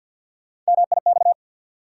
Morse code, heard as a keyed tone, about 53 wpm, 700 Hz. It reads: MIX
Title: Sentences from Top 1100 Words - New Word Drill Alt - 50wpm